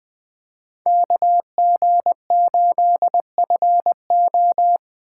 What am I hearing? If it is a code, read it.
Morse code, heard as a keyed tone, about 20 wpm, 700 Hz. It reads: KG8FO
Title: USA Call Signs - 5 Characters - 2x Character Spacing - 20wpm